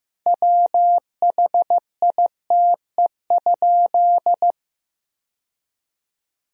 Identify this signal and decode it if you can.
Morse code, heard as a keyed tone, about 15 wpm, 700 Hz. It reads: WHITE?
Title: Sentences from Top 1000 Words - Review Part 1 - 15wpm